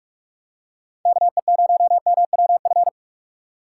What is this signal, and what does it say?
Morse code, heard as a keyed tone, about 47 wpm, 700 Hz. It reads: KE0MWF
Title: USA Call Signs - 6 Characters - 6x Character Spacing - 45wpm